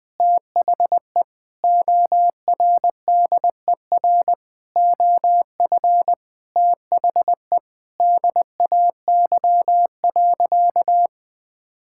Morse code, heard as a keyed tone, about 21 wpm, 700 Hz. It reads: THE ORDER OF THE DAY.
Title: Sentences from Top 1000 Words - Review Part 5 - 20wpm